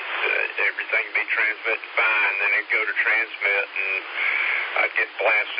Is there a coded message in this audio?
Yes: Emergency Alert System (SAME) tones from 1.1 to 4.7 s, heard as harsh buzzing bursts, with speech.